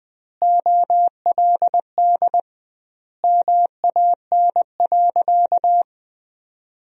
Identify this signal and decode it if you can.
Morse code, heard as a keyed tone, about 21 wpm, 700 Hz. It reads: OLD MAN.
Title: 2x Word-Spacing - Sentences from Top 500 Words - 20wpm